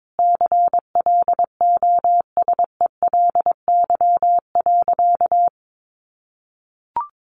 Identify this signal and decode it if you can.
Morse code, heard as a keyed tone, about 22 wpm, 700 Hz. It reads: CLOSELY.
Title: Sentences from Top 1400 Words 22wpm